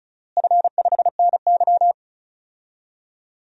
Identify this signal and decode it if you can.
Morse code, heard as a keyed tone, about 36 wpm, 700 Hz. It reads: F5NY
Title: DXCC-100 Call Signs - 4 Characters - Repeated 2x - 35wpm